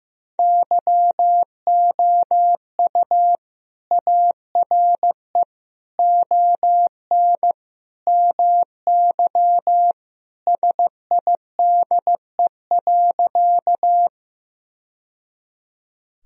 Morse code, heard as a keyed tone, about 16 wpm, 700 Hz. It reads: YOU ARE ON MY SIDE.
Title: Sentences from Top 1000 Words - Review Part 1 - 15wpm